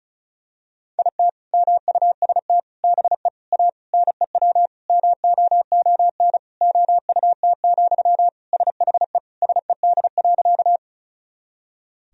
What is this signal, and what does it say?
Morse code, heard as a keyed tone, about 34 wpm, 700 Hz. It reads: IT MUST BE A NEW MOON OUT, SHE SED.
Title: Sentences from Top 500 Words and Abbreviations 35wpm